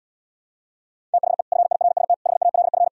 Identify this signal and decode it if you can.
Morse code, heard as a keyed tone, about 77 wpm, 700 Hz. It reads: THE FIRST HALF